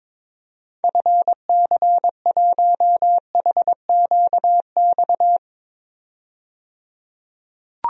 Morse code, heard as a keyed tone, about 21 wpm, 700 Hz. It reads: FC1HQX
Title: DXCC-100 Call Signs - 6 Characters - 22wpm